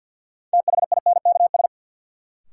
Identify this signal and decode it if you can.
Morse code, heard as a keyed tone, about 53 wpm, 700 Hz. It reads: THINKS
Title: Sentences from Top 1000 Words - Review Part 6 - New Word Drill - 50wpm